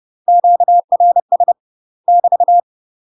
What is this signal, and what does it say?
Morse code, heard as a keyed tone, about 30 wpm, 700 Hz. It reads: QRS <BT>
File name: UK MASTER QSOs TRAINING / 30WPM / UK MASTER QSOs 0014